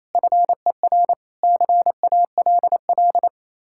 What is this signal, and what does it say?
Morse code, heard as a keyed tone, about 37 wpm, 700 Hz. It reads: FER CALL